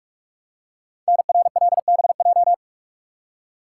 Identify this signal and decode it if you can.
Morse code, heard as a keyed tone, about 44 wpm, 700 Hz. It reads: NRLBJ
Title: Code Groups - 5 Letters - 45wpm